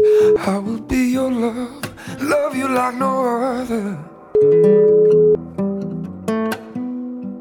Ringback tone (Europe, 425 Hz), with noise.